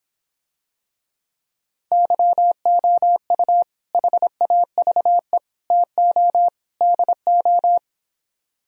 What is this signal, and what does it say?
Morse code, heard as a keyed tone, about 26 wpm, 700 Hz. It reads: YOU HAVE TO DO